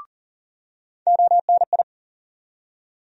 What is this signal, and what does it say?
Morse code, heard as a keyed tone, about 39 wpm, 700 Hz. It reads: ONI
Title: USA Call Sign Postfixes - 3 Letters 40wpm